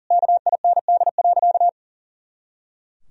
Morse code, heard as a keyed tone, about 37 wpm, 700 Hz. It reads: KIND.
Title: Sentences from Top 500 Words and Abbreviations 40wpm